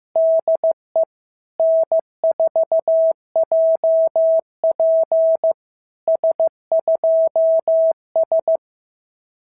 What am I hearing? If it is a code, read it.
Morse code, heard as a keyed tone, about 14 wpm, 650 Hz. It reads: DE N4JP S2S
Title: SOTA QSO Practice 15wpm